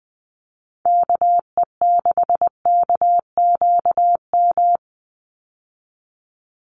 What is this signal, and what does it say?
Morse code, heard as a keyed tone, about 20 wpm, 700 Hz. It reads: KE6KQM
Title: USA Call Signs 20wpm